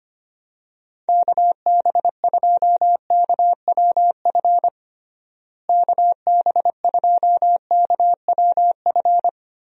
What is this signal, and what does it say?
Morse code, heard as a keyed tone, about 25 wpm, 700 Hz. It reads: KB2KWF KB2KWF
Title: USA Call Signs - 6 Characters - Repeated 2x - 25wpm